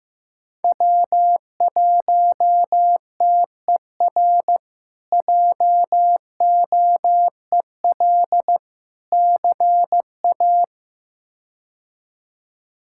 Morse code, heard as a keyed tone, about 16 wpm, 700 Hz. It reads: W1TER JOEL CA
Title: K1USN SST QSO Practice - 15wpm